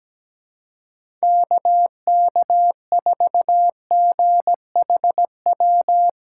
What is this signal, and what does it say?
Morse code, heard as a keyed tone, about 17 wpm, 700 Hz. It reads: KK4GHW
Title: USA Call Signs - 6 Characters - 2x Character Spacing - 17wpm